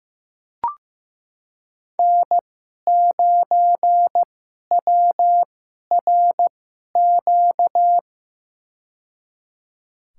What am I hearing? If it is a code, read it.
Morse code, heard as a keyed tone, about 15 wpm, spaced as for 11 wpm, 700 Hz. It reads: N9WRQ
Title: USA Call Signs - 5 Characters - 2x Character Spacing - 15wpm